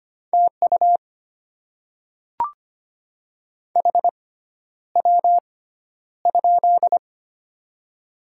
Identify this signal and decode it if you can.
Morse code, heard as a keyed tone, about 25 wpm, spaced as for 11 wpm, 700 Hz. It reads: TU HW?
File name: Top 20 QSO Elements - 6x Character Spacing - 25wpm